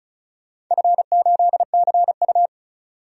Morse code, heard as a keyed tone, about 36 wpm, 690 Hz. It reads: F8CU
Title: DXCC-100 Call Signs - 4 Characters - 35wpm